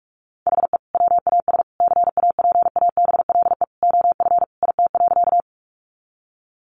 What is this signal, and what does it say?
Morse code, heard as a keyed tone, about 44 wpm, 700 Hz. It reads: HE WAS CAPABLE OF IT.
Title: Sentences from Top 1200 Words 45wpm